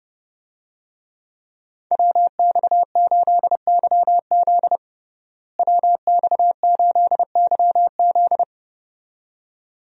Morse code, heard as a keyed tone, about 32 wpm, 700 Hz. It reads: WX8YZ WX8YZ
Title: USA Call Signs - 5 Characters - Repeated 2x - 30wpm